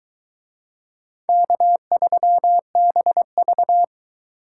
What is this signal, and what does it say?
Morse code, heard as a keyed tone, about 22 wpm, 700 Hz. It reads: K3BV